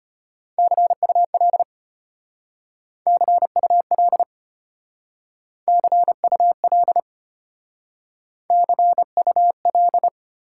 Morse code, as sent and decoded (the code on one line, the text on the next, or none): -.-. ..- .-.. / -.-. ..- .-.. / -.-. ..- .-.. / -.-. ..- .-..
CUL CUL CUL CUL